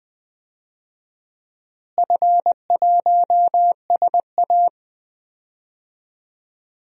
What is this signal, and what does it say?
Morse code, heard as a keyed tone, about 20 wpm, 700 Hz. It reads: F1SA
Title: DXCC-100 Call Signs - 4 Characters - Repeated 2x - 20wpm